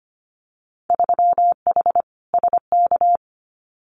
Morse code, heard as a keyed tone, about 25 wpm, 700 Hz. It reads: 3H SK
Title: Winter Field Day Exchanges 25wpm